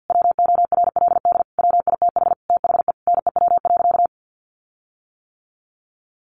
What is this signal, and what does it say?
Morse code, heard as a keyed tone, about 50 wpm, 700 Hz. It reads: WORLD WITH THE NEW.
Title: Sentences from Top 1300 Words 50wpm